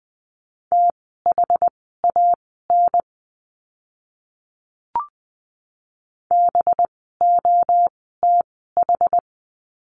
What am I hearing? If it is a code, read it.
Morse code, heard as a keyed tone, about 20 wpm, spaced as for 14 wpm, 700 Hz. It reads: THAN BOTH
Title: 185 Lesson 30f - Letter b - Words 20wpm